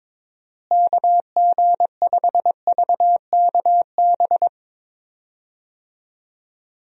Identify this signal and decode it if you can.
Morse code, heard as a keyed tone, about 22 wpm, 700 Hz. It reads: KG5VKB
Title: USA Call Signs - 6 Characters - Repeated 3x - 22wpm